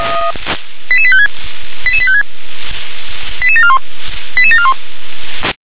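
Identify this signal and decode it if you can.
Five-tone selective call (ZVEI), heard as quick runs of five tones, tones of about 70 ms. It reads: 88856 88856 88741 88741
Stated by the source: ZVEI five-tone selective call received by SDR, address 88765.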